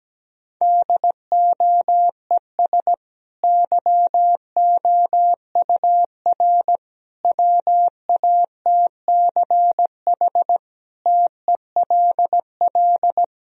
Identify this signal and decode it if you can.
Morse code, heard as a keyed tone, about 17 wpm, 700 Hz. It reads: DOES YOUR WATCH TELL
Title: Sentences from Top 1000 Words - Review Part 1 - 17wpm